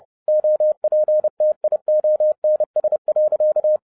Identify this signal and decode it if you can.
Morse code, heard as a keyed tone, about 30 wpm, 610 Hz. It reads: OPTIONS.
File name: 2011-10-mp3 / n10-Czech Army Morse Keys 30wpm